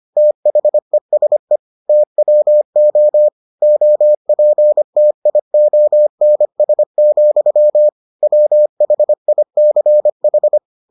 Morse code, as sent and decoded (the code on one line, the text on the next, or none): - .... ... / - .-- --- / --- .--. - .. --- -. ... --..-- / .-- .... .. -.-. ....
THESE TWO OPTIONS, WHICH